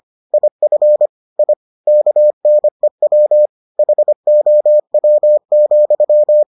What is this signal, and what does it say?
Morse code, heard as a keyed tone, about 25 wpm, 600 Hz. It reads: IF I KNEW HOW,